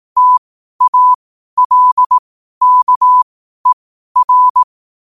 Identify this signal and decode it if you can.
Morse code, heard as a keyed tone, about 18 wpm, spaced as for 12 wpm, 1000 Hz. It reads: TALKER
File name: qotdfiles / 12wpm